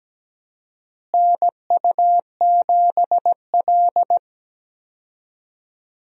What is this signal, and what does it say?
Morse code, heard as a keyed tone, about 17 wpm, 700 Hz. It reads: NU7L
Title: USA Call Signs - 4 Characters - Repeated 3x - 17wpm